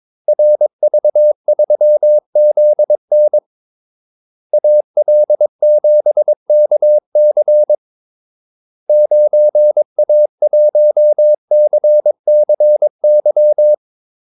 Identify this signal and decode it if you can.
Morse code, heard as a keyed tone, about 22 wpm, 600 Hz. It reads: RV3ZN AL7KC 9A1CCY
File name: Callsigns-22wpm